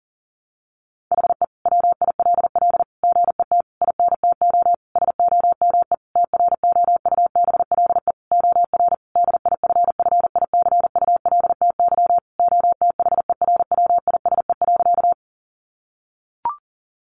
Morse code, heard as a keyed tone, about 40 wpm, 700 Hz. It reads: HE WILL GET INTO SOME TROUBLE OR DIFFICULTY OTHERWISE.
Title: Sentences from Top 1500 Words 40wpm